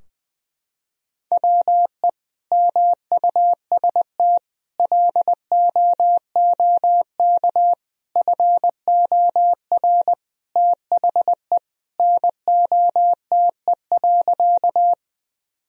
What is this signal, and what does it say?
Morse code, heard as a keyed tone, about 20 wpm, 700 Hz. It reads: WE MUST LOOK FOR THE NOTE.
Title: Sentences from Top 1000 Words - Review Part 2 - 20wpm